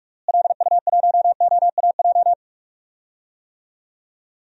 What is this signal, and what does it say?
Morse code, heard as a keyed tone, about 47 wpm, 700 Hz. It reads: LU1OAJ
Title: DXCC-100 Call Signs - 6 Characters - Repeated 2x - 45wpm